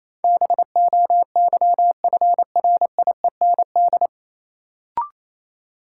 Morse code, heard as a keyed tone, about 28 wpm, 700 Hz. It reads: BOYFRIEND
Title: Sentences from Top 1000 Words - Review Part 5 - New Word Drill - 28wpm